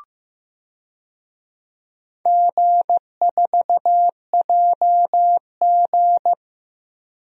Morse code, heard as a keyed tone, about 15 wpm, 700 Hz. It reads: G4JG
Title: DXCC-100 Call Signs - 4 Characters - 15wpm